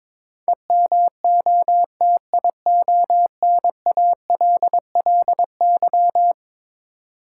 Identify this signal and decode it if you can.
Morse code, heard as a keyed tone, about 22 wpm, 700 Hz. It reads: EMOTIONALLY
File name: practice / 22wpm / Eleven Letter Words - 22wpm